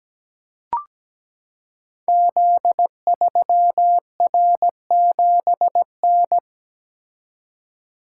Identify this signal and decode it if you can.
Morse code, heard as a keyed tone, about 17 wpm, 690 Hz. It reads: Z3R7N
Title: Code Groups - 5 Letter:Numbers - 17wpm